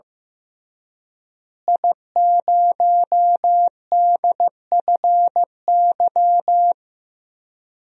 Morse code, heard as a keyed tone, about 15 wpm, 700 Hz. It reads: I0DFY
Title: DXCC-100 Call Signs - 5 Characters - Repeated 3x - 15wpm